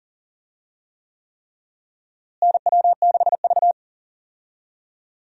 Morse code, heard as a keyed tone, about 40 wpm, 700 Hz. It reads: NW6V
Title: USA Call Signs - 4 Characters - Repeated 3x - 40wpm